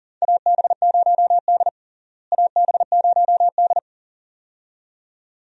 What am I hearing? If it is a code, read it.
Morse code, heard as a keyed tone, about 40 wpm, 700 Hz. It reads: AB0D AB0D
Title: USA Call Signs - 4 Characters - Repeated 2x - 40wpm